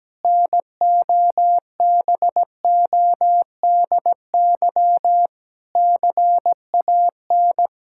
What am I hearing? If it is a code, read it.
Morse code, heard as a keyed tone, about 17 wpm, 700 Hz. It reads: NOBODY CAN
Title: Sentences from Top 1600 Words 17wpm